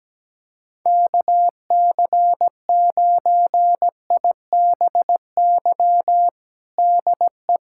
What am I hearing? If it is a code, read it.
Morse code, heard as a keyed tone, about 17 wpm, 700 Hz. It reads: KC9IBY DE
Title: USA Callsign DE Callsign - 6 Characters - 17wpm